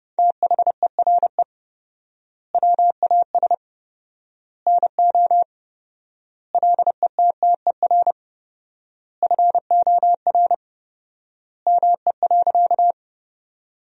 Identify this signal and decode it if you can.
Morse code, heard as a keyed tone, about 30 wpm, 700 Hz. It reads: THERE WAS NO LETTER FOR ME.
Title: 4x Word-Spacing - Sentences from Top 300 Words - 30wpm